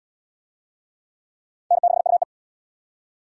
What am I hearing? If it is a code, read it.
Morse code, heard as a keyed tone, about 71 wpm, 700 Hz. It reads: N6UE